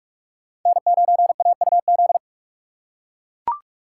Morse code, heard as a keyed tone, about 43 wpm, 700 Hz. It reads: N9AUZ
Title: USA Call Signs - 5 Characters - 4x Character Spacing - 45wpm